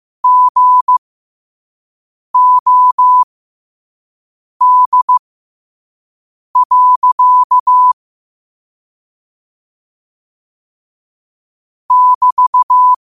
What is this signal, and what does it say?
Morse code, heard as a keyed tone, about 15 wpm, spaced as for 5 wpm, 1000 Hz. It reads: GOD. <BT>
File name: qotdfiles / 5wpm